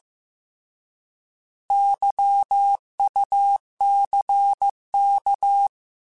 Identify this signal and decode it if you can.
Morse code, heard as a keyed tone, about 15 wpm, 800 Hz. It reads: YUCK
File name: Alphabetical / Alphabetical mp3s / Ys @13